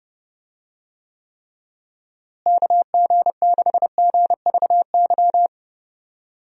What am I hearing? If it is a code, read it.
Morse code, heard as a keyed tone, about 30 wpm, 700 Hz. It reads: KG6GVY
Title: USA Call Signs - 6 Characters - Repeated 3x - 30wpm